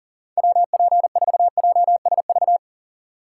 Morse code, heard as a keyed tone, about 40 wpm, 700 Hz. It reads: WP4JSV
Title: USA Call Signs - 6 Characters - 4x Character Spacing - 40wpm